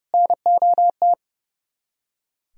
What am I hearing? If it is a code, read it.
Morse code, heard as a keyed tone, about 37 wpm, 700 Hz. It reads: NOT
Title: Speed-Racing - Sets of 4 Words 30wpm